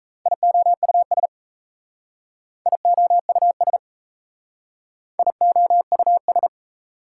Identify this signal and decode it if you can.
Morse code, heard as a keyed tone, about 36 wpm, 700 Hz. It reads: IOUS IOUS IOUS